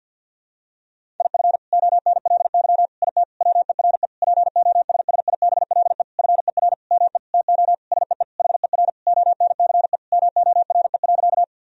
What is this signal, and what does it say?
Morse code, heard as a keyed tone, about 47 wpm, 700 Hz. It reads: IF ONLY IT WERE POSSIBLE FER ME TO SEE HER ONCE MORE.
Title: Sentences from Top 500 Words and Abbreviations 50wpm